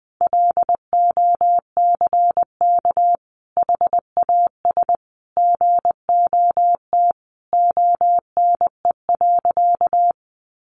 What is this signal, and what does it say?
Morse code, heard as a keyed tone, about 20 wpm, 700 Hz. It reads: LOCK HAS GOT ONE.
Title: Sentences from Top 1300 Words 20wpm